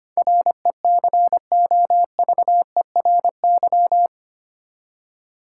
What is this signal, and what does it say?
Morse code, heard as a keyed tone, about 25 wpm, 700 Hz. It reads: RECOVERY
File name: Eight Letter Words - 25wpm